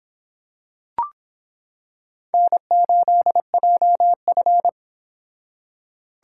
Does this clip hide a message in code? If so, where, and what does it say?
Yes: Morse code from 2.3 to 4.7 s, heard as a keyed tone, about 25 wpm, 700 Hz. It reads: N8JF